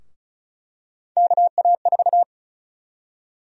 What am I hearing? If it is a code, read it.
Morse code, heard as a keyed tone, about 36 wpm, 700 Hz. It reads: KA4
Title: USA Call Sign Prefixes - 35wpm